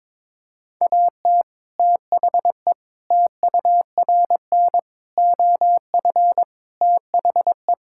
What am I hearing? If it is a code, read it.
Morse code, heard as a keyed tone, about 22 wpm, 700 Hz. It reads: AT THE TURN OF THE
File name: practice / 22wpm / Build A Phrase - 5-Word Phrases - 22wpm